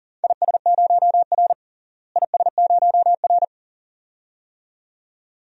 Morse code, heard as a keyed tone, about 37 wpm, 700 Hz. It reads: IS0R IS0R
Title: DXCC-100 Call Signs - 4 Characters - Repeated 2x - 40wpm